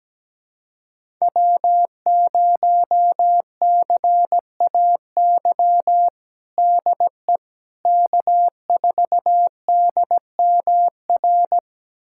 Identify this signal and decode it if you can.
Morse code, heard as a keyed tone, about 17 wpm, 700 Hz. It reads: W0CAY DE K4DMR
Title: USA Callsign DE Callsign - 5 Characters - 17wpm